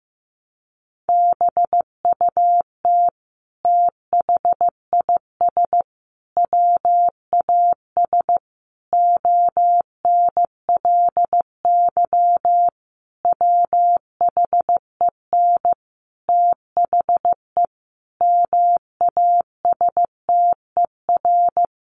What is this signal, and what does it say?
Morse code, heard as a keyed tone, about 15 wpm, 700 Hz. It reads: BUT THIS WAS ONLY WHEN THE MASTER
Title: Sentences from Top 1300 Words 15wpm